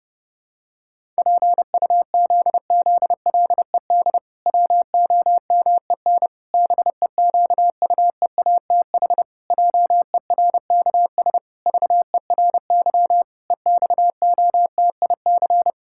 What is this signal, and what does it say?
Morse code, heard as a keyed tone, about 30 wpm, 700 Hz. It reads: PUZZLED WOMEN BEQUEATH JERKS VERY EXOTIC
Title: Pangram Challenge v2 - 30wpm